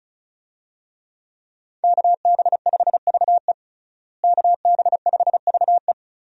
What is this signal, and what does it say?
Morse code, heard as a keyed tone, about 36 wpm, 700 Hz. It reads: KB5VE KB5VE
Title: USA Call Signs - 5 Characters - Repeated 3x - 35wpm